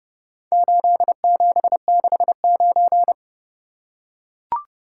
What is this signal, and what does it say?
Morse code, heard as a keyed tone, about 29 wpm, 700 Hz. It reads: 8769